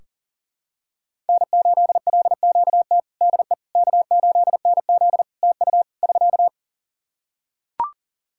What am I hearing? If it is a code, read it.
Morse code, heard as a keyed tone, about 40 wpm, 700 Hz. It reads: N8LQT DE K8NZ TU <SK>
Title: Practice QSO - Exchange 4 - Ending - 40wpm